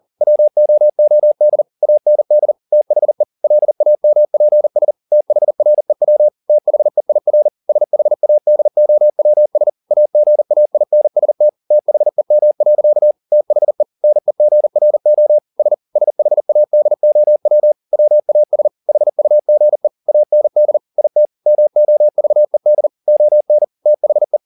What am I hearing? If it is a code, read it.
Morse code, heard as a keyed tone, about 40 wpm, 600 Hz. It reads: WOOD AND THE LAMPS THREW THEIR SHADOWS AGAINST THEM. THE NEGRO S SHADOW WAS HUGE AND IT MOVED ON THE
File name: the old man and the sea 2 40wpm